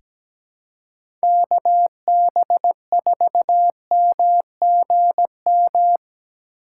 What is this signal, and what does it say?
Morse code, heard as a keyed tone, about 17 wpm, 700 Hz. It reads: KB4MGM
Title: USA Call Signs - 2x Character Spacing - 17wpm